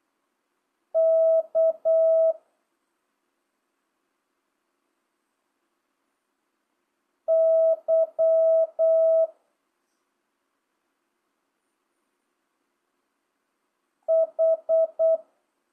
Morse code, heard as a keyed tone, about 8 wpm, 1300 Hz. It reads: K Y H
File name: jh8cbh / 2025cw